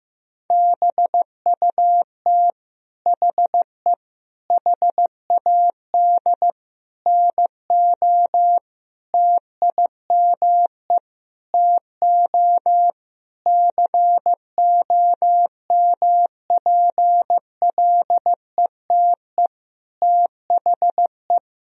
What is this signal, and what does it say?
Morse code, heard as a keyed tone, about 15 wpm, 700 Hz. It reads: BUT HE HAD NO TIME TO COMPLETE THE
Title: Sentences from Top 1000 Words - Review Part 1 - 15wpm